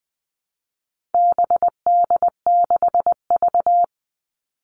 Morse code, heard as a keyed tone, about 20 wpm, 700 Hz. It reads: BD6V
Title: All Cut Numbers - 4-Digit 20wpm